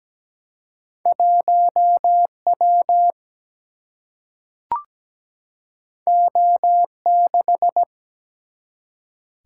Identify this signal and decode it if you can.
Morse code, heard as a keyed tone, about 17 wpm, 700 Hz. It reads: 1 W O 6
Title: Single-Letter Single-Number 17wpm